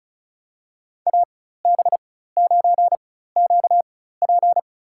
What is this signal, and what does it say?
Morse code, heard as a keyed tone, about 34 wpm, spaced as for 16 wpm, 700 Hz. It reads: AB9QP
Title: USA Call Signs - 5 Characters - 4x Character Spacing - 35wpm